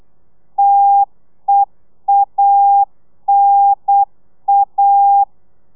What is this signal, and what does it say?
Morse code, heard as a keyed tone, about 8 wpm, 800 Hz. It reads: TEANA